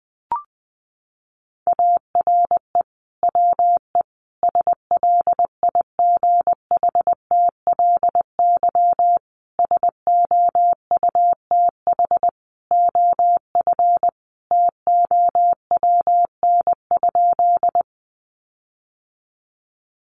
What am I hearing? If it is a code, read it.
Morse code, heard as a keyed tone, about 20 wpm, 700 Hz. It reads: ARE WE SLIGHTLY SOUTH OF TOWN?
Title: Sentences from Top 1300 Words 20wpm